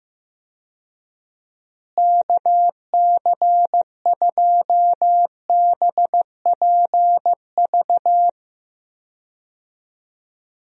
Morse code, heard as a keyed tone, about 15 wpm, 700 Hz. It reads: KC2BPV
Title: USA Call Signs - 6 Characters - Repeated 2x - 15wpm